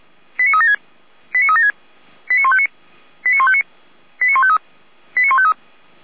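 Five-tone selective call (ZVEI), heard as quick runs of five tones, tones of about 70 ms. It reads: 89376 89376 89159 89159 89153 89153
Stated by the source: ZVEI five-tone selective call received with an SDR.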